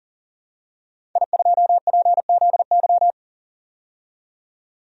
Morse code, heard as a keyed tone, about 40 wpm, 700 Hz. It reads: I2PZY
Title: DXCC-100 Call Signs - 5 Characters - 40wpm